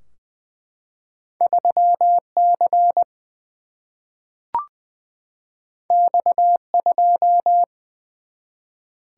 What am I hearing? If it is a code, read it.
Morse code, heard as a keyed tone, about 21 wpm, 700 Hz. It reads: 3 C X 2
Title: Single-Letter Single-Number 20wpm